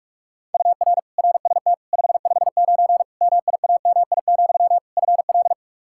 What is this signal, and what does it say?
Morse code, heard as a keyed tone, about 46 wpm, 700 Hz. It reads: UR RST 559 MIAMI, FL
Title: K1USN SST QSO Practice 45wpm